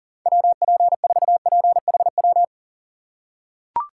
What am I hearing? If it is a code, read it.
Morse code, heard as a keyed tone, about 40 wpm, 700 Hz. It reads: WP4PHW
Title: USA Call Signs - 4x Character Spacing - 40wpm